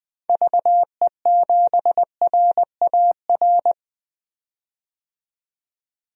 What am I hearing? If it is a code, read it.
Morse code, heard as a keyed tone, about 20 wpm, 700 Hz. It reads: VE7RAR
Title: DXCC-100 Call Signs - 6 Characters - 20wpm